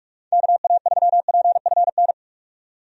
Morse code, heard as a keyed tone, about 46 wpm, 700 Hz. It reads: KA3PFN
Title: USA Call Signs - 6 Characters - 45wpm